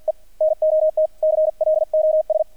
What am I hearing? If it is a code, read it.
Morse code, heard as a keyed tone, about 48 wpm, 630 Hz, with radio noise. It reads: E NOT CLOS
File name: archived / CRSnet